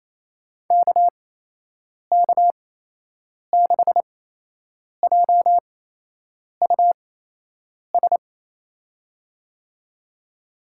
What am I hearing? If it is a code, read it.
Morse code, heard as a keyed tone, about 28 wpm, spaced as for 8 wpm, 700 Hz. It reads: KK6JUS
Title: USA Call Signs - 6 Characters - 8x Character Spacing - 28wpm